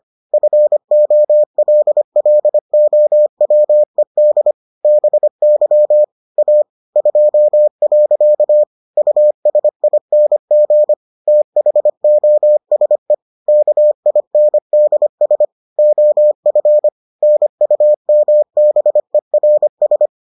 Morse code, heard as a keyed tone, about 25 wpm, 600 Hz. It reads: FOLLOWED BY A 2. USING THOSE KINDS OF NUMBERS